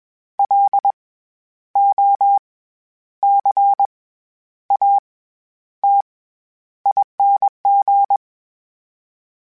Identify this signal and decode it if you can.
Morse code, heard as a keyed tone, about 20 wpm, spaced as for 10 wpm, 800 Hz. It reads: LOCATING
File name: Special INGs-mp3s / Special INGs-mp3s / SlowToFaster ing @20